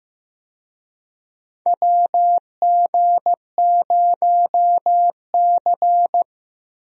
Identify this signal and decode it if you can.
Morse code, heard as a keyed tone, about 15 wpm, 700 Hz. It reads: WG0C
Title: USA Call Signs - 4 Characters - Repeated 2x - 15wpm